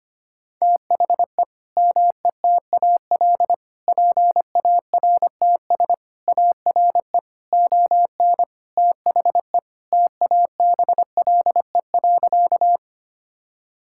Morse code, heard as a keyed tone, about 25 wpm, 700 Hz. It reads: THE METAL PARTS ARE ON THE TABLE.
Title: Sentences from Top 1000 Words - Review Part 6 - 25wpm